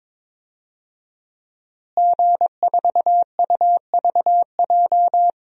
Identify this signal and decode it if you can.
Morse code, heard as a keyed tone, about 21 wpm, 700 Hz. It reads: G4UVJ
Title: DXCC-100 Call Signs - 5 Characters - Repeated 3x - 22wpm